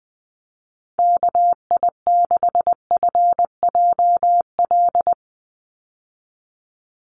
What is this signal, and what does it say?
Morse code, heard as a keyed tone, about 20 wpm, 700 Hz. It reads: KI6FJL
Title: USA Call Signs 20wpm